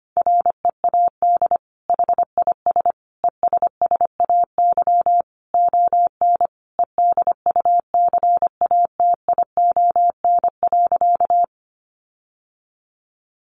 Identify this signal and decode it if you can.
Morse code, heard as a keyed tone, about 25 wpm, 700 Hz. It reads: READ HIS ESSAY ON EDUCATION.
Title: Sentences from Top 1500 Words 25wpm